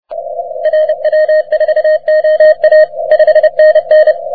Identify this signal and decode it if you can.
Morse code, heard as a keyed tone, about 30 wpm, 620 Hz, with radio noise. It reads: RW4OA 5NN
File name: wp / et3ja 30m cw